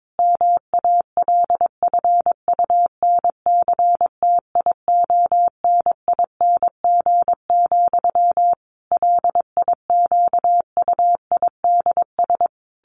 Morse code, heard as a keyed tone, about 22 wpm, 700 Hz. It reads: MALFUNCTIONING, LIQUIDS
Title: A Sound Of Thunder 22wpm